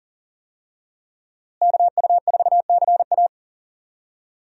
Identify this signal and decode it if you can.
Morse code, heard as a keyed tone, about 40 wpm, 700 Hz. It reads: KU4CA
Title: USA Call Signs - 5 Characters - Repeated 3x - 40wpm